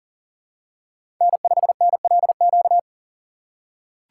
Morse code, heard as a keyed tone, about 37 wpm, 700 Hz. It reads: N5NLQ
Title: USA Call Signs - 5 Characters - 3x Character Spacing - 40wpm